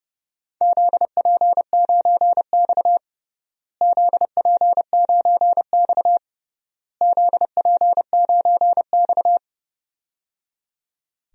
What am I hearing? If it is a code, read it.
Morse code, heard as a keyed tone, about 30 wpm, 700 Hz. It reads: ZP9X ZP9X ZP9X
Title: DXCC-100 Call Signs - 4 Characters - Repeated 3x - 30wpm